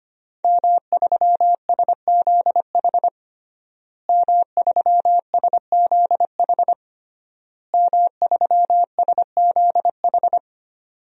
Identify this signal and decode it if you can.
Morse code, heard as a keyed tone, about 25 wpm, 700 Hz. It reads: M3SZH M3SZH M3SZH
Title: DXCC-100 Call Signs - 5 Characters - Repeated 3x - 25wpm